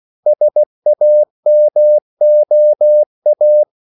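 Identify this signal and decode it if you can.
Morse code, heard as a keyed tone, about 16 wpm, 600 Hz. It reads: SAMOA
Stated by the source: CW practice of national capitals at 16 words per minute, from ON6ZQ.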